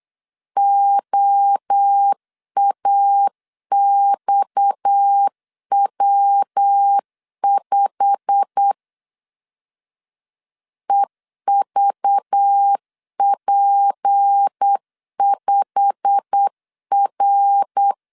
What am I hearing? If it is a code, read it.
Morse code, heard as a keyed tone, about 8 wpm, 800 Hz. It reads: OAXW5 EVP5R
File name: LEZIONI / 8 YJ45 / 8 YJ45 8wpm 3214